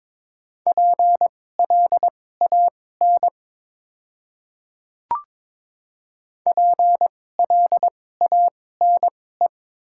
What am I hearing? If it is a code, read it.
Morse code, heard as a keyed tone, about 22 wpm, spaced as for 16 wpm, 690 Hz. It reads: PLAN PLANE
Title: 2x Character Spacing - Top 500 Words - 22wpm